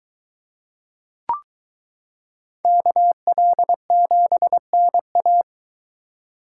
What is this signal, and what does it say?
Morse code, heard as a keyed tone, about 22 wpm, 700 Hz. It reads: KL7NA